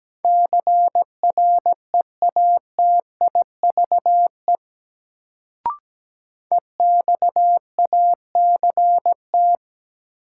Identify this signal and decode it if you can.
Morse code, heard as a keyed tone, about 17 wpm, 700 Hz. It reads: CREATIVE EXACT
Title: Sentences from Top 1300 Words - New Word Drill Alt - 17wpm